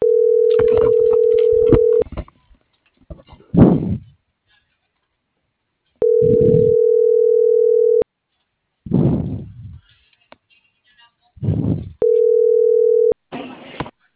Ringback tone (North America), 440 + 480 Hz.
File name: CENTRAL GOMSA / out garcia